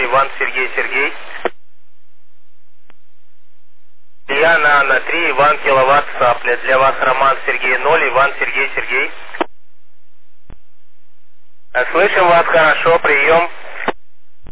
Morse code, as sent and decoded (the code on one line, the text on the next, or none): none